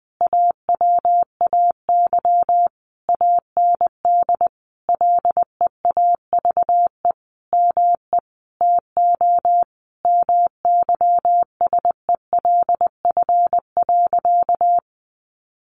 Morse code, heard as a keyed tone, about 20 wpm, 700 Hz. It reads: AWAY AND LEAVE ME TO MYSELF.
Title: Sentences from Top 1300 Words 20wpm